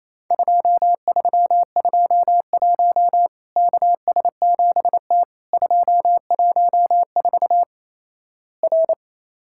Morse code, heard as a keyed tone, about 28 wpm, 700 Hz. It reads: 2321 KS7T 214 R
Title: Worked All Europe DX Contest - QTC Exchange - 28wpm